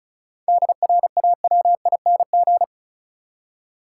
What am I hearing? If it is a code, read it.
Morse code, heard as a keyed tone, about 34 wpm, 700 Hz. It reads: DRAWING
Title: Seven Letter Words - 35wpm